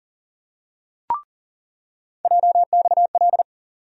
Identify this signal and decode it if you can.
Morse code, heard as a keyed tone, about 40 wpm, 700 Hz. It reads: JXL
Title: USA Call Sign Postfixes - 3 Letters 40wpm